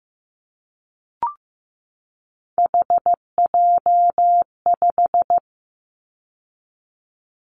Morse code, heard as a keyed tone, about 15 wpm, 700 Hz. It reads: HJ5
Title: Callsign Prefixes for Top 100 DXCC - 15wpm